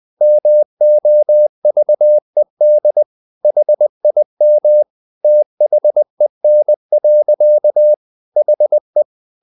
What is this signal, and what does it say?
Morse code, heard as a keyed tone, about 20 wpm, 600 Hz. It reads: MOVED HIM THEN. HE
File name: the old man and the sea part 2 20wpm